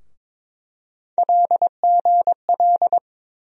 Morse code, heard as a keyed tone, about 22 wpm, 700 Hz. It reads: LGL
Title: USA Call Sign Postfixes - 3 Letters 22wpm